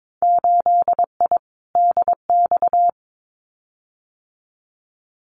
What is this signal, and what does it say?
Morse code, heard as a keyed tone, about 22 wpm, 700 Hz. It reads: 8I DX
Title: Winter Field Day Exchanges Encore 22wpm